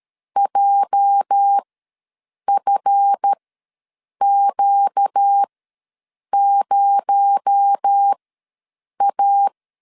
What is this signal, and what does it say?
Morse code, heard as a keyed tone, about 13 wpm, spaced as for 7 wpm, 800 Hz. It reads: JFQ0A
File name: LEZIONI / 8 YJ45 / 8 YJ ws 12wpm 3342